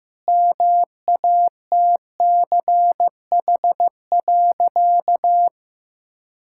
Morse code, heard as a keyed tone, about 16 wpm, 700 Hz. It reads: MATCH.
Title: Sentences from Top 1600 Words 15wpm